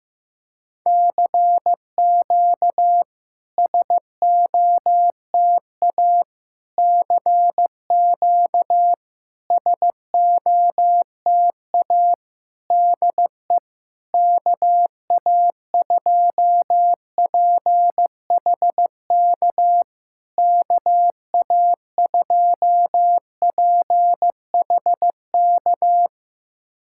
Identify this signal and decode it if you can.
Morse code, heard as a keyed tone, about 15 wpm, 700 Hz. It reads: CQ SOTA CQ SOTA DE KA2PHK KA2PHK
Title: SOTA QSO Practice 15wpm